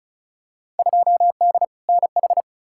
Morse code, heard as a keyed tone, about 36 wpm, 700 Hz. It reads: 2D NH
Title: Field Day Exchanges Encore - 35wpm